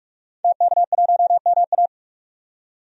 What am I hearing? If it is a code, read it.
Morse code, heard as a keyed tone, about 44 wpm, 700 Hz. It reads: TK1MA
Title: DXCC-100 Call Signs - 5 Characters - 45wpm